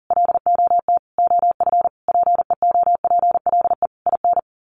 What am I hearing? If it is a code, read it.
Morse code, heard as a keyed tone, about 40 wpm, 700 Hz. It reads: LOT OF PEOPLE IN
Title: Sets of 5 Words 40wpm